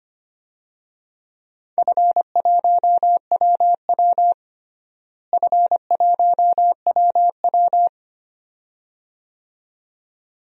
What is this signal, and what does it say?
Morse code, heard as a keyed tone, about 25 wpm, 700 Hz. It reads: F1WW F1WW
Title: DXCC-100 Call Signs - 4 Characters - Repeated 2x - 25wpm